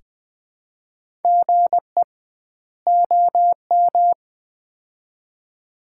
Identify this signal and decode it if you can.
Morse code, heard as a keyed tone, about 20 wpm, 700 Hz. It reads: GE OM
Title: Common QSO Phrases - 2x Word-Spacing - 20wpm